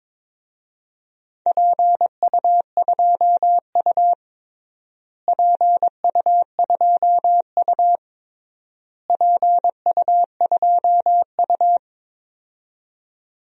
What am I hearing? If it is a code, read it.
Morse code, heard as a keyed tone, about 22 wpm, 700 Hz. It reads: PU2U PU2U PU2U
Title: DXCC-100 Call Signs - 4 Characters - Repeated 3x - 22wpm